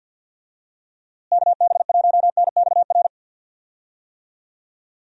Morse code, heard as a keyed tone, about 50 wpm, 700 Hz. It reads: KB1NXR
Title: USA Call Signs - 6 Characters - Repeated 3x - 50wpm